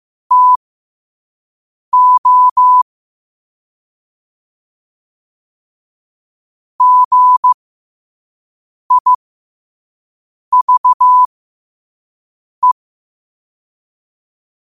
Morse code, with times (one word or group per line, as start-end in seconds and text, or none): 0.3-2.8 TO
6.8-12.7 GIVE